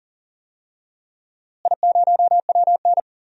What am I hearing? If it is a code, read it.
Morse code, heard as a keyed tone, about 40 wpm, 700 Hz. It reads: I0WN
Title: DXCC-100 Call Signs - 4 Characters - Repeated 3x - 40wpm